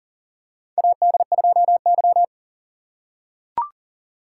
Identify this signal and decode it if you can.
Morse code, heard as a keyed tone, about 40 wpm, 700 Hz. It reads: AD2Y
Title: USA Call Signs - 6x Character Spacing - 40wpm